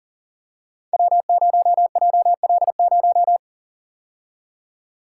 Morse code, heard as a keyed tone, about 41 wpm, 700 Hz. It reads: W0JL0
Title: Code Groups - 5 Letter:Numbers - 40wpm